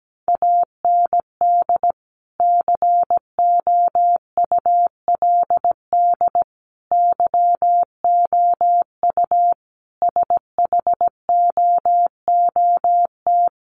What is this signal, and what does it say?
Morse code, heard as a keyed tone, about 17 wpm, 700 Hz. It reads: AND COULD YOU SHOOT
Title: Sentences from Top 1200 Words 17wpm